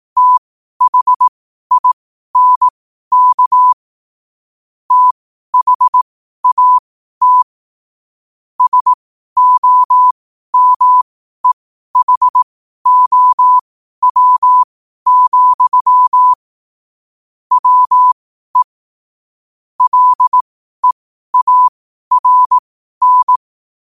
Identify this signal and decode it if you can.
Morse code, heard as a keyed tone, about 18 wpm, spaced as for 12 wpm, 1000 Hz. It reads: THINK THAT SOMEHOW, WE LEARN
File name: qotdfiles / 12wpm